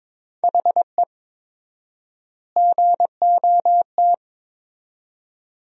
Morse code, heard as a keyed tone, about 22 wpm, 700 Hz. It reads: HE GOT
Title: 4x Word-Spacing - Sentences from Top 400 Words - 22wpm